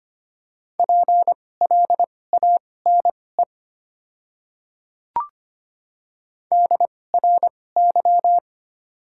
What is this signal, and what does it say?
Morse code, heard as a keyed tone, about 25 wpm, spaced as for 18 wpm, 700 Hz. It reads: PLANE DRY
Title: 2x Character Spacing - Top 500 Words - 25wpm